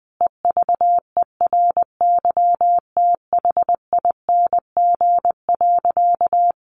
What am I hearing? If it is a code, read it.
Morse code, heard as a keyed tone, about 20 wpm, 700 Hz. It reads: EVERYTHING.
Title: Sentences from Top 700 Words 20wpm